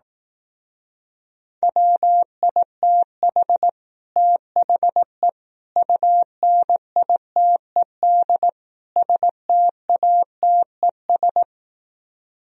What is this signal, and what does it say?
Morse code, heard as a keyed tone, about 18 wpm, 700 Hz. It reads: WITH THE UNITED STATES
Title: Speed-Racing - Sets of 4 Words 15wpm